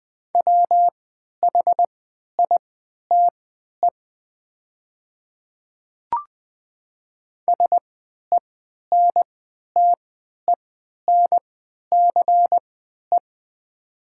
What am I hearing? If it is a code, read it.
Morse code, heard as a keyed tone, about 20 wpm, spaced as for 11 wpm, 700 Hz. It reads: WHITE SENTENCE
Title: 3x Character Spacing - Top 500 Words - 20wpm